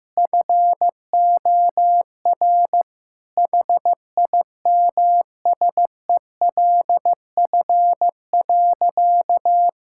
Morse code, heard as a keyed tone, about 16 wpm, 700 Hz. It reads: FOR HIMSELF.
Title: Sentences from Top 1600 Words 15wpm